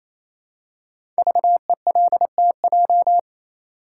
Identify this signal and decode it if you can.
Morse code, heard as a keyed tone, about 27 wpm, 700 Hz. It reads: VELTJ
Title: Code Groups - 5 Letter:Numbers - 28wpm